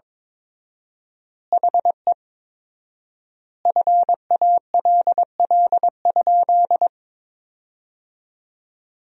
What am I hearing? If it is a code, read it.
Morse code, heard as a keyed tone, about 22 wpm, 700 Hz. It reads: HE FALL?
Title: 4x Word-Spacing - Sentences from Top 500 Words - 22wpm